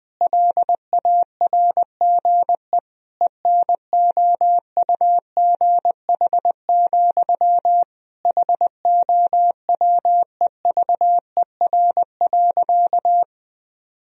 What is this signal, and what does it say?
Morse code, heard as a keyed tone, about 19 wpm, 700 Hz. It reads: LARGE ENOUGH, HOWEVER.
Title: Sentences from Top 1000 Words - Review Part 1 - 20wpm